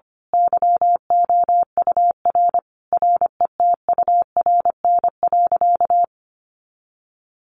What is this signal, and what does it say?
Morse code, heard as a keyed tone, about 25 wpm, 700 Hz. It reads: YOUR RETURN.
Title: A Sound Of Thunder 25wpm